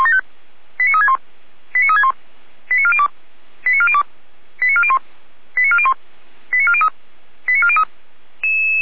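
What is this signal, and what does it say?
Five-tone selective call (ZVEI), heard as quick runs of five tones, tones of about 70 ms. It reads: 89361 89361 89402 89402 89401 89401 89403 89403
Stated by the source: ZVEI five-tone call received by SDR.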